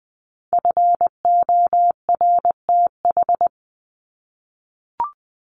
Morse code, heard as a keyed tone, about 20 wpm, 700 Hz. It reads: FORTH
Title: Top 1751-2000 Words 20wpm